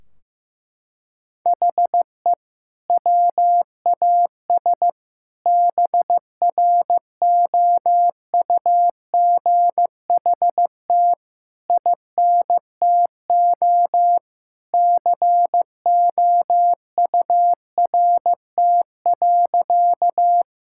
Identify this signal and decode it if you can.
Morse code, heard as a keyed tone, about 16 wpm, 700 Hz. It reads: HE WAS BROUGHT INTO COURT.
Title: Sentences from Top 1600 Words 15wpm